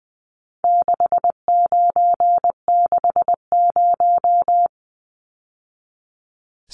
Morse code, heard as a keyed tone, about 20 wpm, 700 Hz. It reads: 6960
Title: Numbers - 4-Digit 20wpm